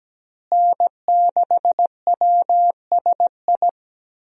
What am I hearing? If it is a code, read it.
Morse code, heard as a keyed tone, about 17 wpm, 700 Hz. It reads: N6WSI
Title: USA Call Signs - 5 Characters - Repeated 2x - 17wpm